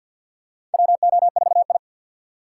Morse code, heard as a keyed tone, about 48 wpm, 700 Hz. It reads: WO4I
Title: USA Call Signs - 4 Characters - 6x Character Spacing - 50wpm